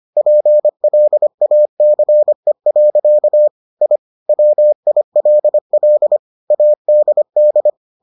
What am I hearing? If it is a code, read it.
Morse code, heard as a keyed tone, about 25 wpm, 600 Hz. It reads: PLACE. I WILL ADD